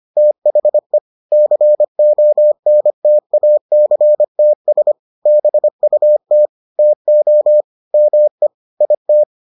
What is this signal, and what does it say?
Morse code, heard as a keyed tone, about 25 wpm, 600 Hz. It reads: THE CONTACTS BUT TO ME IT